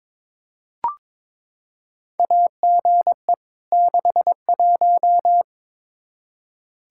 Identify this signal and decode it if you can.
Morse code, heard as a keyed tone, about 22 wpm, 700 Hz. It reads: AGE 61
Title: K1USN SST QSO Practice 22wpm